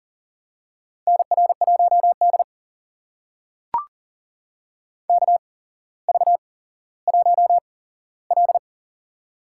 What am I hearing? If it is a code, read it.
Morse code, heard as a keyed tone, about 42 wpm, spaced as for 17 wpm, 700 Hz. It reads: NR1D KV1L
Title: USA Call Signs - 4 Characters - 8x Character Spacing - 40wpm